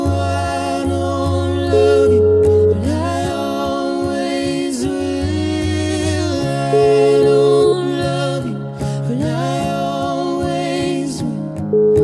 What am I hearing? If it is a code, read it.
Ringback tone (Europe, 425 Hz), with noise.